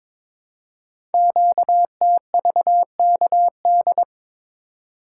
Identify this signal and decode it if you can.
Morse code, heard as a keyed tone, about 22 wpm, 700 Hz. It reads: QTVKD
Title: Code Groups - 5 Letter:Numbers - 22wpm